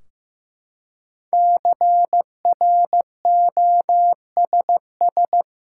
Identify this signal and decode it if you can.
Morse code, heard as a keyed tone, about 15 wpm, 700 Hz. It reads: CROSS
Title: Five Letter Words - 15wpm